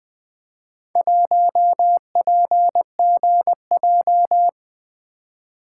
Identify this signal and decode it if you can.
Morse code, heard as a keyed tone, about 20 wpm, 700 Hz. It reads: 1PGJ
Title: Four Letter-Number Combinations - 20wpm